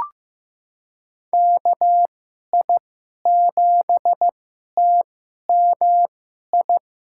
Morse code, heard as a keyed tone, about 16 wpm, spaced as for 11 wpm, 700 Hz. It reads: KI7TMI
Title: USA Call Signs - 2x Character Spacing - 15wpm